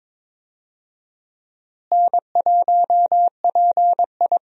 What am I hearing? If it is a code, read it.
Morse code, heard as a keyed tone, about 22 wpm, 700 Hz. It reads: N1PI
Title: USA Call Signs - 4 Characters - Repeated 2x - 22wpm